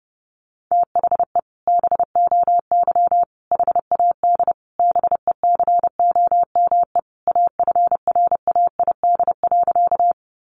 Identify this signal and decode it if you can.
Morse code, heard as a keyed tone, about 30 wpm, 700 Hz. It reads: THE BOY HAD BECOME AFRAID.
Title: Sentences from Top 1500 Words 30wpm